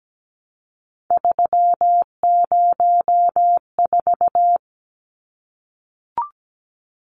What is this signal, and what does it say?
Morse code, heard as a keyed tone, about 17 wpm, 700 Hz. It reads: 304
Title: Numbers - 3-Digit 17wpm